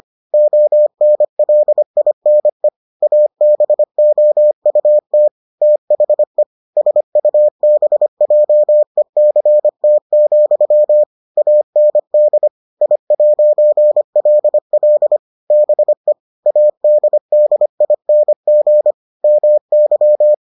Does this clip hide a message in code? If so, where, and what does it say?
Yes: Morse code from 0.3 to 20.3 s, heard as a keyed tone, about 25 wpm, 600 Hz. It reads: ONLINE ABOUT THE SUBJECT, AND I'LL BE ADDING MY